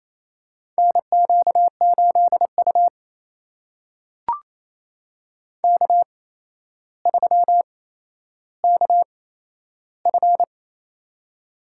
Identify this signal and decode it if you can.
Morse code, heard as a keyed tone, about 28 wpm, spaced as for 12 wpm, 700 Hz. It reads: NQ8U K3KF
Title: USA Call Signs - 4 Characters - 8x Character Spacing - 28wpm